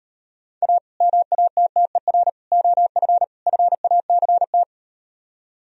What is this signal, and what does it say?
Morse code, heard as a keyed tone, about 38 wpm, 700 Hz. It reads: A MATTER OF FACT